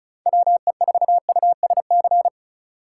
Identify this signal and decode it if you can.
Morse code, heard as a keyed tone, about 36 wpm, 700 Hz. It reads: WE4USC
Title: Speed Racing - USA 2x3 Call Signs 30wpm